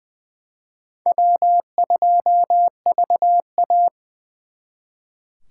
Morse code, heard as a keyed tone, about 24 wpm, 700 Hz. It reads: W2VA
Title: Speed Racing - USA 1x2 Call Signs 20wpm